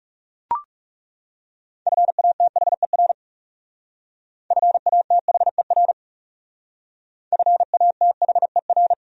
Morse code, heard as a keyed tone, about 35 wpm, 700 Hz. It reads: FATHER FATHER FATHER
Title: Speed-Racing Top 200 Words 30wpm